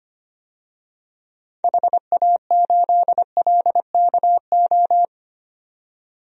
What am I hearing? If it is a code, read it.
Morse code, heard as a keyed tone, about 25 wpm, 700 Hz. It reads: HA8LKO
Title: DXCC-100 Call Signs - 6 Characters - 25wpm